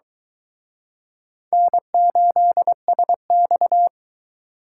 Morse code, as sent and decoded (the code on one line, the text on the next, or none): -. ---.. ... -..-
N8SX